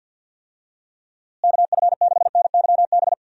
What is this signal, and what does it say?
Morse code, heard as a keyed tone, about 52 wpm, 700 Hz. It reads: KF6NYB
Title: USA Call Signs - 6 Characters - Repeated 3x - 50wpm